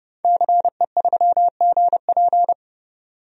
Morse code, heard as a keyed tone, about 31 wpm, 700 Hz. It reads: CE3GP
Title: DXCC-100 Call Signs - 5 Characters - 30wpm